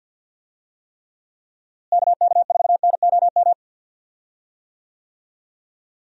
Morse code, heard as a keyed tone, about 52 wpm, 700 Hz. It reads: KK4NOM
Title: USA Call Signs - 6 Characters - Repeated 3x - 50wpm